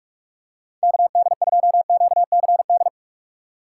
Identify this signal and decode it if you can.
Morse code, heard as a keyed tone, about 44 wpm, 700 Hz. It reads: KD2QCD